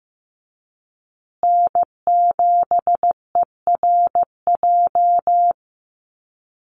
Morse code, heard as a keyed tone, about 15 wpm, 700 Hz. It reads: N7ERJ
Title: USA Call Signs Repeated 2x 15wpm